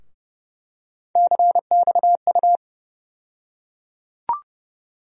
Morse code, heard as a keyed tone, about 30 wpm, 700 Hz. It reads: CXU
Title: USA Call Sign Postfixes - 3 Letters 30wpm